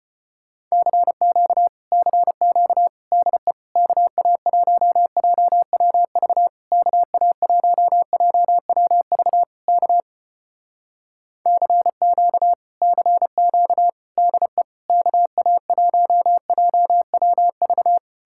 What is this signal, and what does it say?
Morse code, heard as a keyed tone, about 34 wpm, 700 Hz. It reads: CQ CQ DE KA1JWV KA1JWV K CQ CQ DE KA1JWV